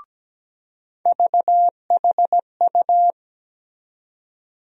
Morse code, heard as a keyed tone, about 17 wpm, 700 Hz. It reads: VHU
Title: USA Call Sign Postfixes - 3 Letters 17wpm